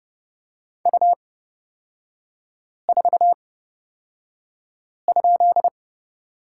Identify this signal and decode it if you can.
Morse code, heard as a keyed tone, about 31 wpm, spaced as for 5 wpm, 700 Hz. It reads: U4?
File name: Lesson 36e - BT - Three Characters 30wpm